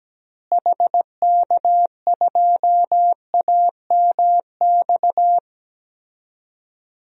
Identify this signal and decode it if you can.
Morse code, heard as a keyed tone, about 17 wpm, 700 Hz. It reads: HK2AMX
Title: DXCC-100 Call Signs - 6 Characters - Repeated 2x - 17wpm